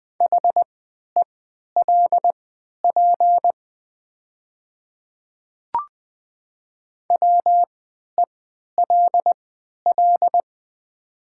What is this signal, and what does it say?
Morse code, heard as a keyed tone, about 20 wpm, spaced as for 11 wpm, 700 Hz. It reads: HELP WELL
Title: 3x Character Spacing - Top 200 Words - 20wpm